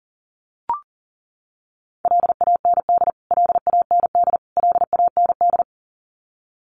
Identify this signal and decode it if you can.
Morse code, heard as a keyed tone, about 40 wpm, 700 Hz. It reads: LAND LAND LAND
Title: Top 500 Words Repeated 40wpm